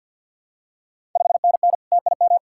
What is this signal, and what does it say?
Morse code, heard as a keyed tone, about 52 wpm, 700 Hz. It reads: 5NN TIM
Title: Ontario QSO Party Exchange 50wpm